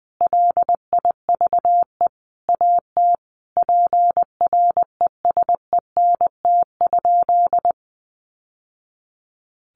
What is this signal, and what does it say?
Morse code, heard as a keyed tone, about 19 wpm, 700 Hz. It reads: LIVE AT PRESENT?
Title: Sentences from Top 800 Words 20wpm